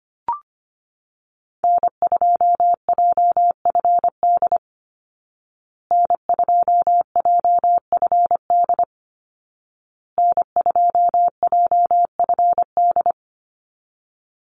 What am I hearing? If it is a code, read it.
Morse code, heard as a keyed tone, about 25 wpm, 700 Hz. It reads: N2JFD N2JFD N2JFD
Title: Call Signs Repeated 2x - 25wpm